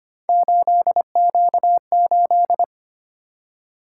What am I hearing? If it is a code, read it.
Morse code, heard as a keyed tone, about 25 wpm, 700 Hz. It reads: 8Q8
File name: Three Letter-Number Combinations - 25wpm